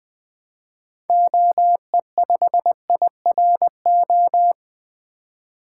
Morse code, heard as a keyed tone, about 20 wpm, 700 Hz. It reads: OE5IRO
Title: 2023 SOTA Activator Call Signs - 20wpm